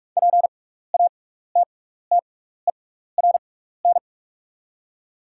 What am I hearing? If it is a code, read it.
Morse code, heard as a keyed tone, about 48 wpm, spaced as for 16 wpm, 690 Hz. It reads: PATTERN
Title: 6x Character Spacing - Top 500 Words - 45wpm